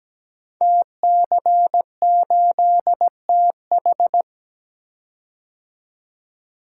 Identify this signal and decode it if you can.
Morse code, heard as a keyed tone, about 17 wpm, 700 Hz. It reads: TC8TH
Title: Code Groups - 5 Letter:Numbers - 17wpm